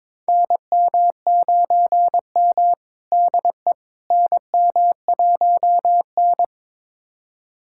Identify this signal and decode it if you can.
Morse code, heard as a keyed tone, about 22 wpm, 700 Hz. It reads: NM9M DE NM1N
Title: USA Callsign DE Callsign - 4 Characters - 22wpm